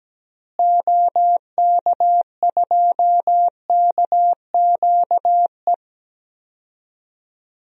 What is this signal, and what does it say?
Morse code, heard as a keyed tone, about 17 wpm, 700 Hz. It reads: OK2KQE
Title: DXCC-100 Call Signs - 6 Characters - Repeated 2x - 17wpm